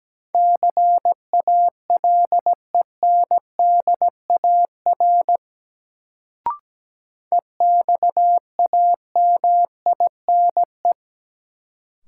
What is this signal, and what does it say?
Morse code, heard as a keyed tone, about 17 wpm, 700 Hz. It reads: CALENDAR EXAMINE
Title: Sentences from Top 1600 Words - New Word Drill Alt - 17wpm